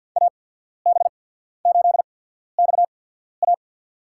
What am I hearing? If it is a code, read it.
Morse code, heard as a keyed tone, about 54 wpm, spaced as for 14 wpm, 700 Hz. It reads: AB8XA
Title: USA Call Signs - 5 Characters - 8x Character Spacing - 50wpm